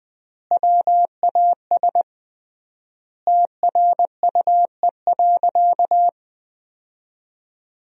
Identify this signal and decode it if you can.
Morse code, heard as a keyed tone, about 21 wpm, 700 Hz. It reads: WAS TRUE.
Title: 3x Word-Spacing - Sentences from Top 400 Words - 20wpm